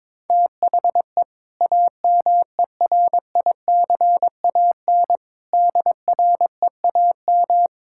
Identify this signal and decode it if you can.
Morse code, heard as a keyed tone, about 22 wpm, 700 Hz. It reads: THE AMERICAN DREAM
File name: practice / 22wpm / Sentences with 2-Word Phrases - 22wpm